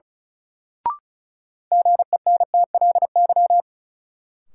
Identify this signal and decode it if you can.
Morse code, heard as a keyed tone, about 36 wpm, 700 Hz. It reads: GENTLY
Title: Sentences from Top 1500 Words - New Word Drill - 35wpm